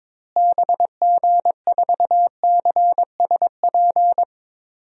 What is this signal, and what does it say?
Morse code, heard as a keyed tone, about 22 wpm, 700 Hz. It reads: BG4CSP
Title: DXCC-100 Call Signs - 6 Characters - Repeated 3x - 22wpm